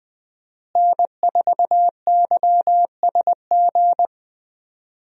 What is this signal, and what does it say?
Morse code, heard as a keyed tone, about 20 wpm, 700 Hz. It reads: N4YSG
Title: USA Call Signs - 5 Characters - Repeated 3x - 20wpm